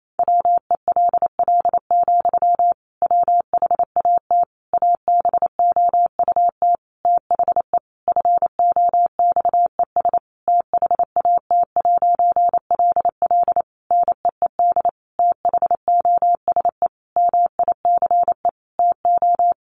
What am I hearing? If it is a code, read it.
Morse code, heard as a keyed tone, about 28 wpm, 700 Hz. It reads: WELL, WHAT ABOUT THE FOXES THAT'LL NEED THOSE MICE TO
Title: A Sound Of Thunder 28wpm